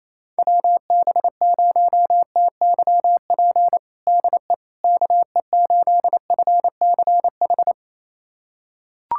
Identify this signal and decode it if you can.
Morse code, heard as a keyed tone, about 28 wpm, 700 Hz. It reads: WB0TYP DE KE8FCH
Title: USA Callsign DE Callsign - 6 Characters - 28wpm